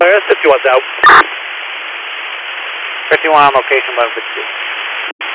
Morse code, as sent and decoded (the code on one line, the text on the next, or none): none